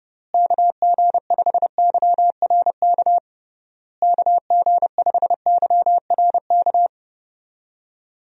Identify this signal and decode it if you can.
Morse code, heard as a keyed tone, about 30 wpm, 700 Hz. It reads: KG5YRK KG5YRK
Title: USA Call Signs - 6 Characters - Repeated 3x - 30wpm